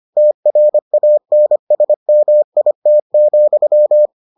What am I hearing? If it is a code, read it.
Morse code, heard as a keyed tone, about 25 wpm, 600 Hz. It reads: TRANSMIT,